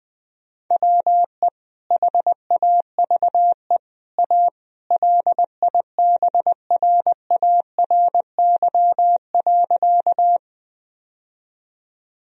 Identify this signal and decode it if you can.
Morse code, heard as a keyed tone, about 20 wpm, 700 Hz. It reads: WE HAVE A LIBRARY.
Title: Sentences from Top 1000 Words - Review Part 4 - 20wpm